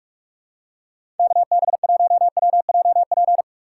Morse code, heard as a keyed tone, about 46 wpm, 700 Hz. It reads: KB1WJP
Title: USA Call Signs - 2x Character Spacing - 45wpm